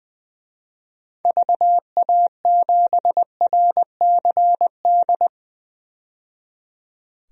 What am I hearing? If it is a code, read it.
Morse code, heard as a keyed tone, about 21 wpm, 700 Hz. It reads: VA7RCD
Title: DXCC-100 Call Signs - 6 Characters - 20wpm